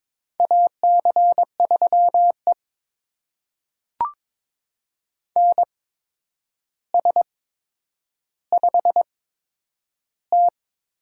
Morse code, heard as a keyed tone, about 22 wpm, spaced as for 9 wpm, 700 Hz. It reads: AC3E NS5T
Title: USA Call Signs - 4 Characters - 8x Character Spacing - 22wpm